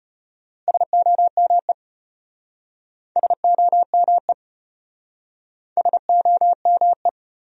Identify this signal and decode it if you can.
Morse code, heard as a keyed tone, about 32 wpm, 700 Hz. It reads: SOME SOME SOME